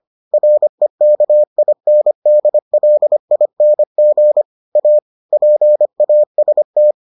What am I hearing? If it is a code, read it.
Morse code, heard as a keyed tone, about 25 wpm, 600 Hz. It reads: REKINDLING A PAST